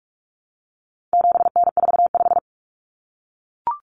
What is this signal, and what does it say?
Morse code, heard as a keyed tone, about 44 wpm, 700 Hz. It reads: 7N45
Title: Common Cut Numbers - 4-Digit 45wpm